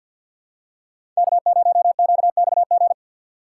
Morse code, heard as a keyed tone, about 48 wpm, 700 Hz. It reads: K0QXG
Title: USA Call Signs - 5 Characters - 3x Character Spacing - 50wpm